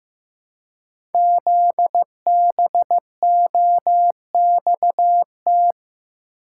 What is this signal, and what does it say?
Morse code, heard as a keyed tone, about 15 wpm, 700 Hz. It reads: ZBOXT
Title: Code Groups - 5 Letters - 15wpm